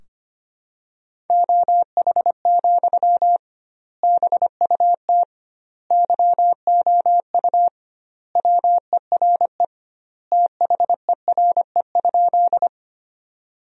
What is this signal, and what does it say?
Morse code, heard as a keyed tone, about 25 wpm, 700 Hz. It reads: OH, BUT YOU WERE THERE?
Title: 2x Word-Spacing - Sentences from Top 500 Words - 25wpm